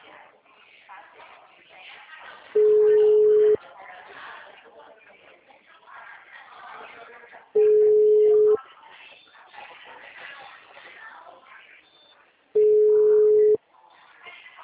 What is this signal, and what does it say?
Ringback tone (Europe, 425 Hz), with speech.